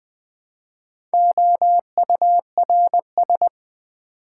Morse code, heard as a keyed tone, about 20 wpm, 700 Hz. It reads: OURS
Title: Build A Word - 1 Letter At A Time - 4-Letter Words - 20wpm